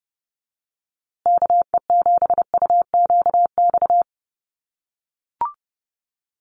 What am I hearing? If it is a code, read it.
Morse code, heard as a keyed tone, about 30 wpm, 700 Hz. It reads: KE7UQX
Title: USA Call Signs Repeated 2x 30wpm